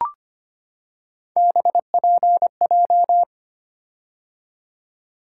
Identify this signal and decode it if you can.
Morse code, heard as a keyed tone, about 25 wpm, 700 Hz. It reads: BPJ
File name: Three Letter-Number Combinations - 25wpm